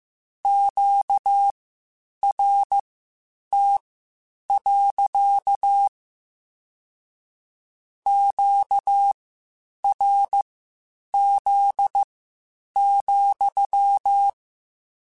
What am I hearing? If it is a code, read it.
Morse code, heard as a keyed tone, about 15 wpm, spaced as for 8 wpm, 800 Hz. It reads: QRT. QRZ,
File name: Q Calls / Q Calls-mp3s / Q Calls sppc @13